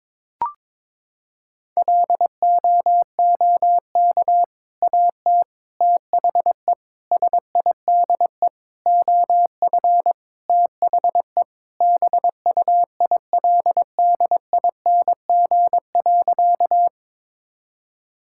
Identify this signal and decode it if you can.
Morse code, heard as a keyed tone, about 22 wpm, 700 Hz. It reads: LOOK AT THE SIDE OF THE BUILDING.
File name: practice / 22wpm / Sentences with 3-Word Phrases - 22wpm